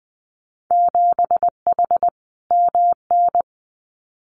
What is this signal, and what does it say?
Morse code, heard as a keyed tone, about 20 wpm, 700 Hz. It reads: 7H MN
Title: Winter Field Day Exchanges Encore 20wpm